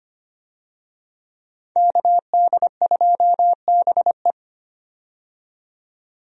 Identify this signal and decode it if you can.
Morse code, heard as a keyed tone, about 25 wpm, 700 Hz. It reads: KD2BE
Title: USA Call Signs - 5 Characters - 25wpm